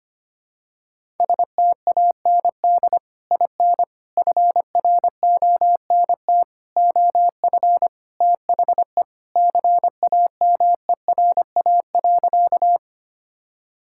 Morse code, heard as a keyed tone, about 25 wpm, 700 Hz. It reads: STAND IN FRONT OF THE CAMERA.
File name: practice / 25wpm / Sentences with 3-Word Phrases - 25wpm